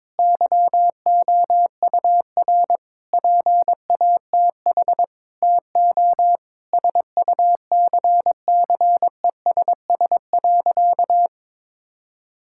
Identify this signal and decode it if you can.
Morse code, heard as a keyed tone, about 21 wpm, 700 Hz. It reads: YOUR PATH TO SUCCESS.
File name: Difficult Sentences with Hints - 22wpm